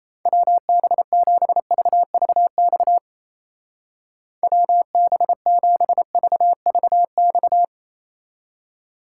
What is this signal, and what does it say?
Morse code, heard as a keyed tone, about 32 wpm, 700 Hz. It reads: WB7VVX WB7VVX
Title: Speed Racing - USA 2x3 Call Signs 28wpm